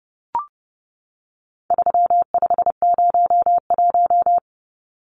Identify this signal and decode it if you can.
Morse code, heard as a keyed tone, about 30 wpm, 700 Hz. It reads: 3501
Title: Numbers - 4-Digit 30wpm